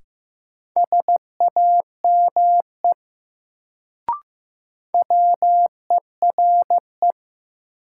Morse code, heard as a keyed tone, about 15 wpm, 700 Hz. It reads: SAME WERE
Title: Sets of 2 Words - New Word Drill - 15wpm